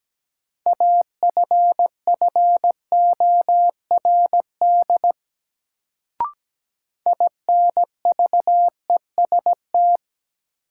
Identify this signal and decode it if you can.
Morse code, heard as a keyed tone, about 17 wpm, 700 Hz. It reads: AFFORD INVEST
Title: Sentences from Top 1100 Words - New Word Drill - 17wpm